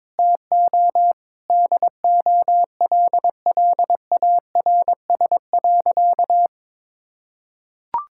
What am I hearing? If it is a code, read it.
Morse code, heard as a keyed tone, about 22 wpm, 700 Hz. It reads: TO DOLLARS.
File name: Sentences with 9-Word Phrases - 22wpm